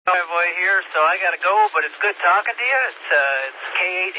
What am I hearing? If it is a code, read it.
Emergency Alert System (SAME) tones, heard as harsh buzzing bursts, with speech. It is heard from 0.5 to 3.5 s.